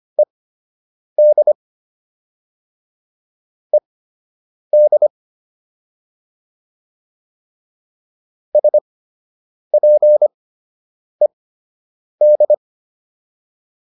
Morse code, heard as a keyed tone, about 25 wpm, spaced as for 8 wpm, 600 Hz. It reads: ED ED SPED